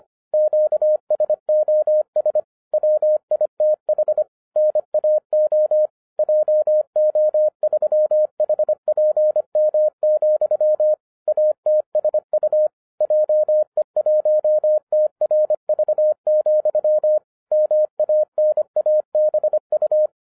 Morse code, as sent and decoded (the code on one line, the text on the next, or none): --.- ... --- ... / .-- .. - .... / -. .- --- / .--- --- ...-- .... .--. -- --..-- / .- - ... ..- / .--- . .---- - .-. ...- --..-- / -- .- -. .- -... ..-
QSOS WITH NAO JO3HPM, ATSU JE1TRV, MANABU